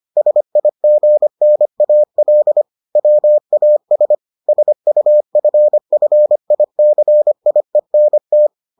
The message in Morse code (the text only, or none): SIGNAL WAS SUFFICIENT